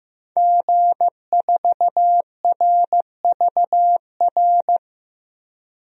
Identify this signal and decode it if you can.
Morse code, heard as a keyed tone, about 15 wpm, 700 Hz. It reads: G4RVR
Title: DXCC-100 Call Signs - 5 Characters - 15wpm